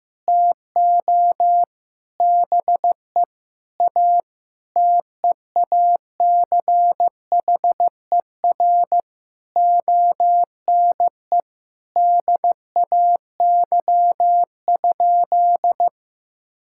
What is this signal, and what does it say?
Morse code, heard as a keyed tone, about 15 wpm, 700 Hz. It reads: TO BE A TEACHER ONE DAY?
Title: Sentences from Top 1000 Words - Review Part 6 - 15wpm